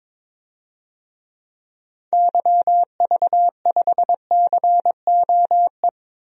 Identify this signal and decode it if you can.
Morse code, heard as a keyed tone, about 22 wpm, 700 Hz. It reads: YV5COE
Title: DXCC-100 Call Signs - 6 Characters - Repeated 3x - 22wpm